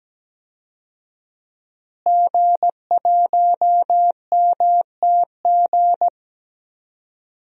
Morse code, heard as a keyed tone, about 17 wpm, 700 Hz. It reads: G1MTG
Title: DXCC-100 Call Signs - 5 Characters - Repeated 2x - 17wpm